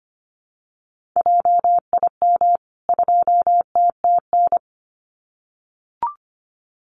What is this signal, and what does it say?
Morse code, heard as a keyed tone, about 25 wpm, 700 Hz. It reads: JIM 2TTN
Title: CWT Exchange 25wpm